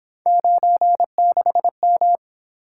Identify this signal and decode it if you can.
Morse code, heard as a keyed tone, about 26 wpm, 700 Hz. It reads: 96M
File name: Speed-Racing - Three Letter-Number Combinations - 22wpm